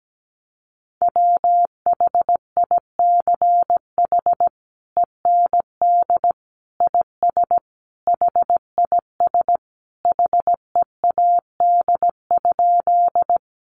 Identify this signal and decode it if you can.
Morse code, heard as a keyed tone, about 17 wpm, 700 Hz. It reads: WHICH END IS HIS HEAD?
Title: Sentences from Top 300 Words 17wpm